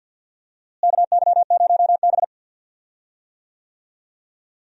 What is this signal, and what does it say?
Morse code, heard as a keyed tone, about 45 wpm, 700 Hz. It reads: KY0B
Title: USA Call Signs - 4 Characters - Repeated 3x - 50wpm